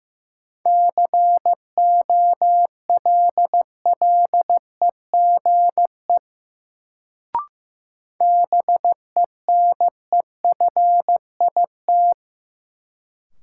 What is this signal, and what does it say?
Morse code, heard as a keyed tone, about 15 wpm, 700 Hz. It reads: COLLEGE BENEFIT
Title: Sentences from Top 1300 Words - New Word Drill Alt - 15wpm